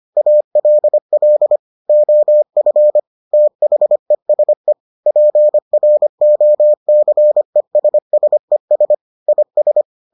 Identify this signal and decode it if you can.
Morse code, heard as a keyed tone, about 25 wpm, 600 Hz. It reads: ALL OF THESE PROCESSES IS